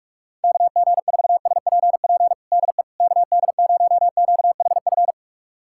Morse code, heard as a keyed tone, about 46 wpm, 700 Hz. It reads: KG4SPP DE KD0QHF
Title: USA Callsign DE Callsign - 6 Characters - 45wpm